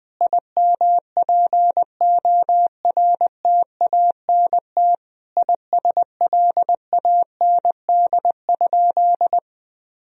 Morse code, heard as a keyed tone, about 20 wpm, 700 Hz. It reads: IMPORTANT ISLAND?
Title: Sentences from Top 1000 Words - Review Part 3 - 20wpm